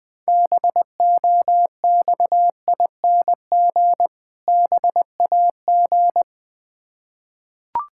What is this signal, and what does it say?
Morse code, heard as a keyed tone, about 20 wpm, 690 Hz. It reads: BOXING BAG
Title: Difficult Words with Hints - 20wpm